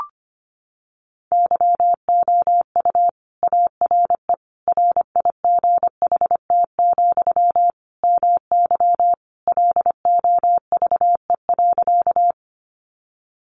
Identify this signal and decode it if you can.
Morse code, heard as a keyed tone, about 25 wpm, 700 Hz. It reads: YOU ARE RIGHT, MY LOVE.
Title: Sentences from Top 400 Words 25wpm